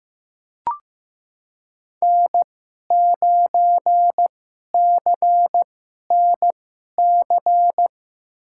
Morse code, heard as a keyed tone, about 15 wpm, spaced as for 11 wpm, 700 Hz. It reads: N9CNC
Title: USA Call Signs - 5 Characters - 2x Character Spacing - 15wpm